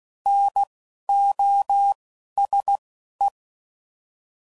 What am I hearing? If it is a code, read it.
Morse code, heard as a keyed tone, about 16 wpm, 800 Hz. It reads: N O S E
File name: Prefixes / By First Letter / Ns